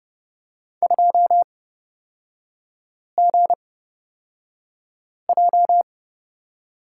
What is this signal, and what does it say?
Morse code, heard as a keyed tone, about 30 wpm, spaced as for 5 wpm, 700 Hz. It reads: 2GJ
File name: Lesson 36e - BT - Three Characters 30wpm